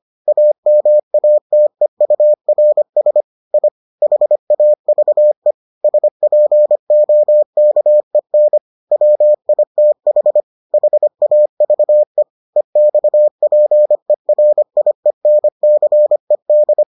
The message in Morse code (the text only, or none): AMATEURS I HAVE SPOKEN WITH HAVE EXPERIENCED